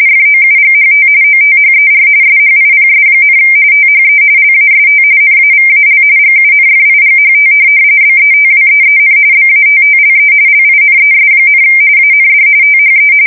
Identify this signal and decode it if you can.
RTTY, about 45.45 baud. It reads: FOUGHT OTHER SHIPS, THE LAST NAVAL GUNFIGHT BETWEEN BATTLE LINES, AND THE
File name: fsk170 / USS-New-Jersey